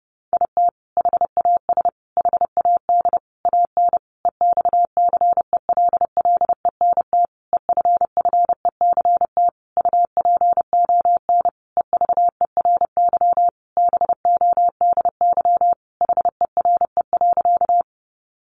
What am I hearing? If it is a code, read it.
Morse code, heard as a keyed tone, about 28 wpm, 700 Hz. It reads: IT HAS HAD AN EXCELLENT EFFECT UPON EVERY BODY HERE.
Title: Sentences from Top 1100 Words 30wpm